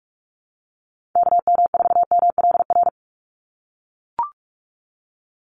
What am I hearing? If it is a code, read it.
Morse code, heard as a keyed tone, about 44 wpm, 700 Hz. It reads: KM4MLR
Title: USA Call Signs 45wpm